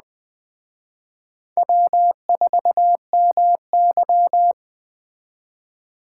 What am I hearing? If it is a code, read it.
Morse code, heard as a keyed tone, about 20 wpm, 700 Hz. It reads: W4MY
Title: Speed Racing - USA 1x2 Call Signs 17wpm